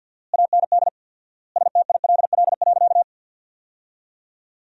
Morse code, heard as a keyed tone, about 51 wpm, 700 Hz. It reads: AND STILL.
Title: 4x Word-Spacing - Sentences from Top 500 Words - 50wpm